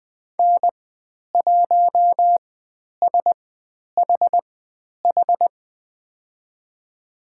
Morse code, heard as a keyed tone, about 20 wpm, spaced as for 10 wpm, 700 Hz. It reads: N1SHH
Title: 102 Lesson 17g - Letter u - Call Signs 20wpm